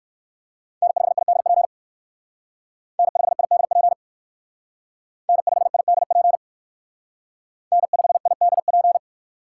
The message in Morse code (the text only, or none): N5IDP N5IDP N5IDP N5IDP